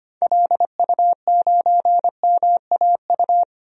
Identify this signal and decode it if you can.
Morse code, heard as a keyed tone, about 25 wpm, 700 Hz. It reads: LU9MAU
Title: DXCC-100 Call Signs - 6 Characters - 25wpm